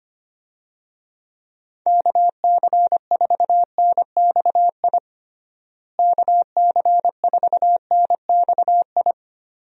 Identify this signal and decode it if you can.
Morse code, heard as a keyed tone, about 25 wpm, 700 Hz. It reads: KC4NXI KC4NXI
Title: USA Call Signs - 6 Characters - Repeated 2x - 25wpm